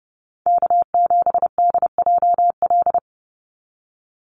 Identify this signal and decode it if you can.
Morse code, heard as a keyed tone, about 30 wpm, 700 Hz. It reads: K7DJL
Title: NA SOTA Chaser Callsigns 30wpm